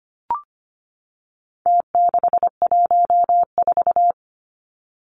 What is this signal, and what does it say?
Morse code, heard as a keyed tone, about 25 wpm, 700 Hz. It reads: T614
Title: Numbers - Contest Style - 4-Digit 25wpm